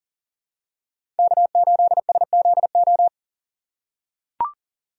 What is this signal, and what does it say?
Morse code, heard as a keyed tone, about 40 wpm, 700 Hz. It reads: K8SZO
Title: USA Call Signs - 5 Characters - 6x Character Spacing - 40wpm